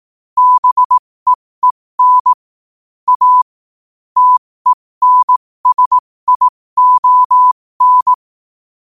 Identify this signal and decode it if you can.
Morse code, heard as a keyed tone, about 18 wpm, spaced as for 15 wpm, 1000 Hz. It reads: BEEN A TENSION
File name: qotdfiles / 15wpm